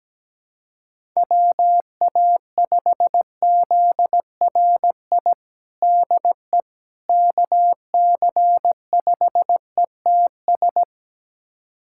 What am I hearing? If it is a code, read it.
Morse code, heard as a keyed tone, about 17 wpm, 700 Hz. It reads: WA5ZRI DE KC5ETS
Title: USA Callsign DE Callsign 17wpm